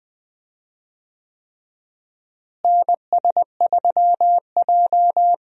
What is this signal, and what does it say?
Morse code, heard as a keyed tone, about 20 wpm, 700 Hz. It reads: NS3J
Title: USA Call Signs - 4 Characters - Repeated 3x - 20wpm